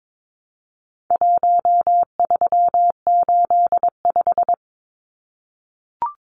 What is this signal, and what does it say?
Morse code, heard as a keyed tone, about 22 wpm, 700 Hz. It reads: 1385
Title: Numbers - 4-Digit 22wpm